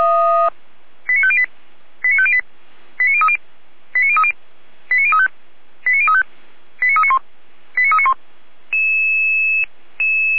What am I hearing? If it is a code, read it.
Five-tone selective call (ZVEI), heard as quick runs of five tones, tones of about 70 ms. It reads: 89508 89508 89030 89030 89035 89035 89391 89391
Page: /455/SDR/ZVEI/